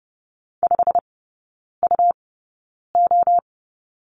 Morse code, heard as a keyed tone, about 30 wpm, spaced as for 9 wpm, 700 Hz. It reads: 5UO
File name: Lesson 18e - Letter m - Three Characters 30wpm